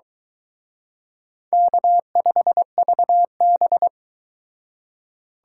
Morse code, heard as a keyed tone, about 20 wpm, 700 Hz. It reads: K5VB
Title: Speed Racing - USA 1x2 Call Signs 17wpm